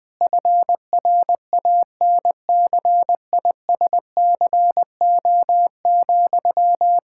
Morse code, heard as a keyed tone, about 20 wpm, 700 Hz. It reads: FRANCISCO,
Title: Practice QSO - Exchange 1 - Beginning - 20wpm